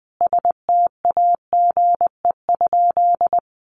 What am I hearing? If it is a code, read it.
Morse code, heard as a keyed tone, about 20 wpm, 700 Hz. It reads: STAGE?
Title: Sentences from Top 1200 Words 20wpm